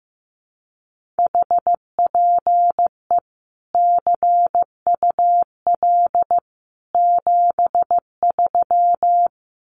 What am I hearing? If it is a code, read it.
Morse code, heard as a keyed tone, about 15 wpm, 700 Hz. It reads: HPE CUL 73
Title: Common QSO Phrases 15wpm